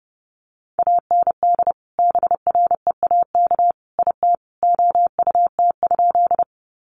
Morse code, heard as a keyed tone, about 30 wpm, 700 Hz. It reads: AND BREAK IT OUT?
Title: Sentences from Top 900 Words 30wpm